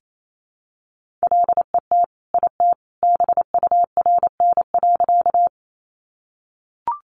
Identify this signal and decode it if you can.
Morse code, heard as a keyed tone, about 28 wpm, 700 Hz. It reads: LET IT BURN.
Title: Sentences from Top 1300 Words 28wpm